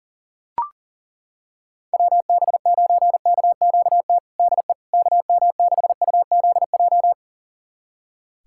Morse code, heard as a keyed tone, about 40 wpm, 700 Hz. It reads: WB9KQT DE KM6UZJ
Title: USA Callsign DE Callsign - 6 Characters - 40wpm